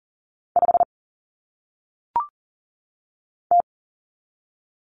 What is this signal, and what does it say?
Morse code, heard as a keyed tone, about 40 wpm, 700 Hz. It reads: 5 T
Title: Common Cut Numbers - 1-Digit 40wpm